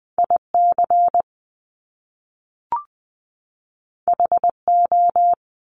Morse code, heard as a keyed tone, about 20 wpm, 700 Hz. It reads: IC HO
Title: Top Two Letter Combinations in Words 20wpm